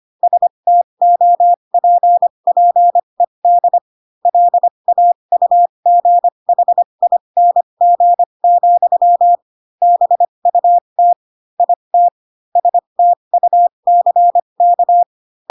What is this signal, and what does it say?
Morse code, heard as a keyed tone, about 25 wpm, spaced as for 22 wpm, 700 Hz. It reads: STOPPED LAUGHING, BUT IT STUCK
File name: Pinocchio / Pinocchio ch3 18 25 ik1hge